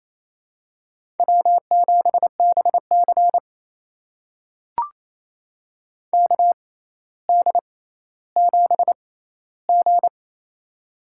Morse code, heard as a keyed tone, about 28 wpm, spaced as for 14 wpm, 700 Hz. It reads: W7BC KD7G
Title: USA Call Signs - 4 Characters - 6x Character Spacing - 28wpm